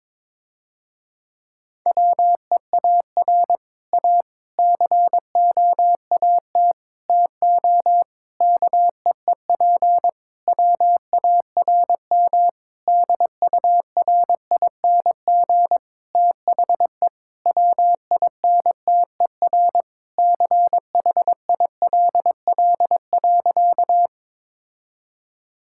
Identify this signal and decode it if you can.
Morse code, heard as a keyed tone, about 22 wpm, 700 Hz. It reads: WEAR A COAT TO KEEP WARM DURING THE WINTER CHILL.
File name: Difficult Sentences with Hints - 22wpm